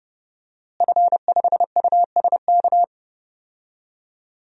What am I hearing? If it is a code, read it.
Morse code, heard as a keyed tone, about 30 wpm, 700 Hz. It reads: F5USK
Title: DXCC-100 Call Signs - 5 Characters - Repeated 2x - 30wpm